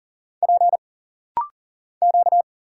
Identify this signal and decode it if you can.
Morse code, heard as a keyed tone, about 39 wpm, 700 Hz. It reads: P Q
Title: Single Letters - Rapid-Fire 40wpm